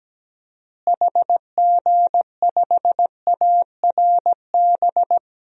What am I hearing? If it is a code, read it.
Morse code, heard as a keyed tone, about 17 wpm, 700 Hz. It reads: HG5ARB
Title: DXCC-100 Call Signs - 6 Characters - Repeated 2x - 17wpm